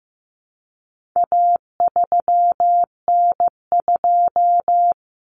Morse code, heard as a keyed tone, about 15 wpm, 700 Hz. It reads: A3N2
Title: Common Cut Numbers - 4-Digit 15wpm